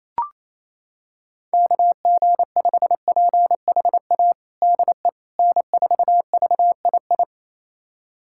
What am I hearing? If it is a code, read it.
Morse code, heard as a keyed tone, about 28 wpm, 700 Hz. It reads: KG5PHA DE N4VII
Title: USA Callsign DE Callsign 28wpm